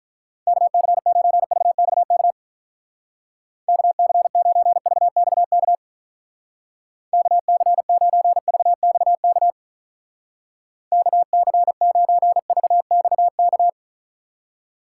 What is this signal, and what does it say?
Morse code, heard as a keyed tone, about 43 wpm, 700 Hz. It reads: KC9VXK KC9VXK KC9VXK KC9VXK
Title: Speed Racing - USA Call Signs 35wpm